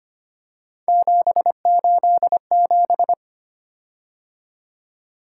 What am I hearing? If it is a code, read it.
Morse code, heard as a keyed tone, about 25 wpm, 700 Hz. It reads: 787
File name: Three Letter-Number Combinations - 25wpm